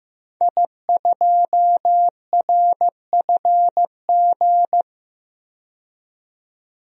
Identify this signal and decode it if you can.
Morse code, heard as a keyed tone, about 15 wpm, 700 Hz. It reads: I2RFG
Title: DXCC-100 Call Signs - 5 Characters - 15wpm